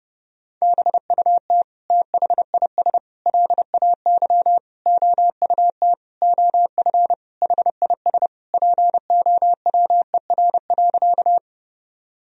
Morse code, heard as a keyed tone, about 30 wpm, 700 Hz. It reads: BUT THIS LAY OUT OF HIS POWER.
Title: Sentences from Top 1000 Words - Review Part 2 - 30wpm